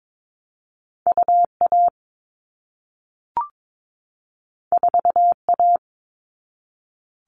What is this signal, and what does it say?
Morse code, heard as a keyed tone, about 22 wpm, 700 Hz. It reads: UA 4A
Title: All Cut Numbers - 2-Digit 22wpm